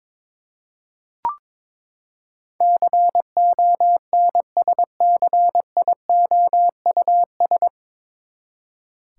Morse code, heard as a keyed tone, about 22 wpm, 700 Hz. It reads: CONSCIOUS
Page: Morse Code Ninja practice 22wpm